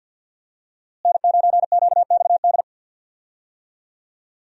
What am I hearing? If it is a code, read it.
Morse code, heard as a keyed tone, about 53 wpm, 700 Hz. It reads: N9QXD
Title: USA Call Signs - 5 Characters - Repeated 2x - 50wpm